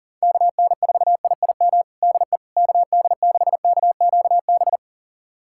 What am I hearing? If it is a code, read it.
Morse code, heard as a keyed tone, about 40 wpm, 700 Hz. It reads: KN4IIM DE KD6KQB
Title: USA Callsign DE Callsign 40wpm